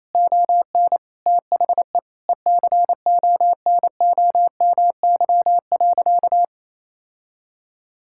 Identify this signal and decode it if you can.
Morse code, heard as a keyed tone, about 28 wpm, 700 Hz. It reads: ON THE ECONOMY.
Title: Sentences from Top 1000 Words - Review Part 6 - 28wpm